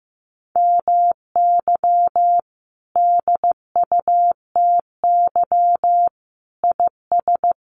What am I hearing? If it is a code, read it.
Morse code, heard as a keyed tone, about 15 wpm, 700 Hz. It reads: MY DUTY IS
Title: Sentences from Top 1500 Words 15wpm